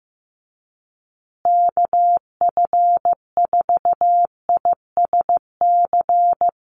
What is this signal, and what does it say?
Morse code, heard as a keyed tone, about 15 wpm, 700 Hz. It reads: KF4ISC
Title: USA Call Signs Repeated 3x 15wpm